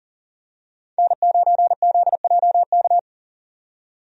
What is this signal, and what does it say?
Morse code, heard as a keyed tone, about 40 wpm, 700 Hz. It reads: N9ZJK
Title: USA Call Signs - 5 Characters - 40wpm